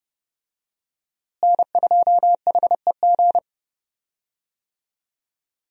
Morse code, heard as a keyed tone, about 30 wpm, 700 Hz. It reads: N2HEG
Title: USA Call Signs - 5 Characters - Repeated 3x - 30wpm